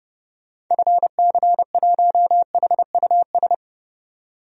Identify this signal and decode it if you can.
Morse code, heard as a keyed tone, about 30 wpm, 700 Hz. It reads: FC1HUS
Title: DXCC-100 Call Signs - 6 Characters - 30wpm